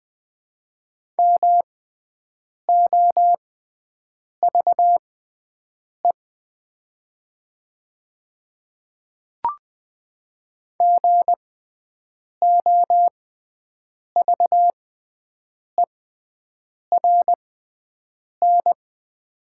Morse code, heard as a keyed tone, about 20 wpm, spaced as for 7 wpm, 700 Hz. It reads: MOVE GOVERN
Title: 6x Character Spacing - Top 500 Words - 20wpm